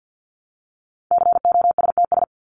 Morse code, heard as a keyed tone, about 50 wpm, 700 Hz. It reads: COSTS
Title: ES:S Suffix - Top 100 Words with Base Words 50wpm